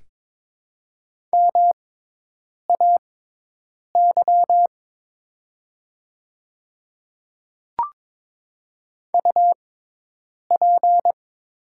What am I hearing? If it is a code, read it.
Morse code, heard as a keyed tone, about 23 wpm, spaced as for 8 wpm, 700 Hz. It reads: MAY UP
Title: 6x Character Spacing - Top 100 Words - 22wpm